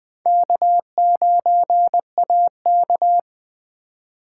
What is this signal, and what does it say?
Morse code, heard as a keyed tone, about 20 wpm, 700 Hz. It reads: K9AK
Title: USA Call Signs - 4x Character Spacing - 20wpm